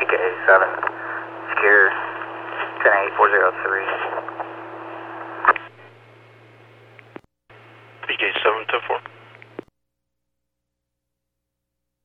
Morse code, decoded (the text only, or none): none